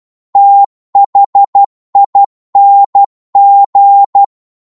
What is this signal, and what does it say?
Morse code, heard as a keyed tone, about 12 wpm, 800 Hz. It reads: THING
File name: Morse / psg-12wpm